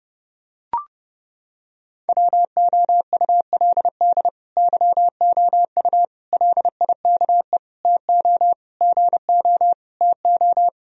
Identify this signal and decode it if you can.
Morse code, heard as a keyed tone, about 30 wpm, 700 Hz. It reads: WOULD YOU LIKE TO GO TO
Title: Sentences from Top 1000 Words - Review Part 1 - 30wpm